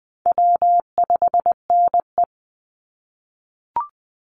Morse code, heard as a keyed tone, about 20 wpm, 700 Hz. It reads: W5NE
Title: Easy Call Signs - 20wpm